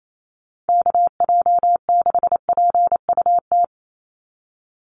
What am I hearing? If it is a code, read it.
Morse code, heard as a keyed tone, about 28 wpm, 700 Hz. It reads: KJ6PUT
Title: USA Call Signs 28wpm